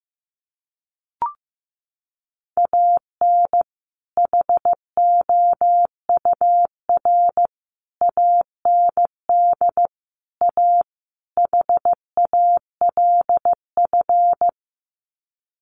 Morse code, heard as a keyed tone, about 15 wpm, 700 Hz. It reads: AN HOUR AND A HALF
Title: Sets of 5 Words 15wpm